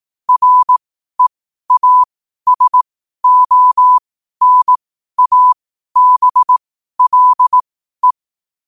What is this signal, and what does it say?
Morse code, heard as a keyed tone, about 18 wpm, spaced as for 12 wpm, 1000 Hz. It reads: REASONABLE
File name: qotdfiles / 12wpm